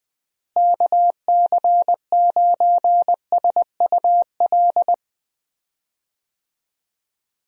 Morse code, heard as a keyed tone, about 21 wpm, 700 Hz. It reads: KC9SUL
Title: USA Call Signs - 6 Characters - Repeated 2x - 20wpm